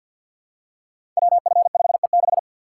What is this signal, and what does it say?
Morse code, heard as a keyed tone, about 51 wpm, 700 Hz. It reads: WF5E6
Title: Code Groups - 5 Letter:Numbers - 50wpm